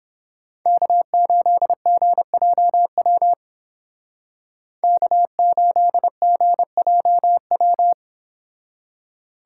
Morse code, heard as a keyed tone, about 29 wpm, 700 Hz. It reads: K8GJW K8GJW